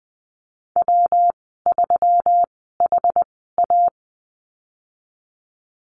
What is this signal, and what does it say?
Morse code, heard as a keyed tone, about 20 wpm, spaced as for 14 wpm, 700 Hz. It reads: W3HA
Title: 179 Lesson 29g - Slash - Call Signs 20wpm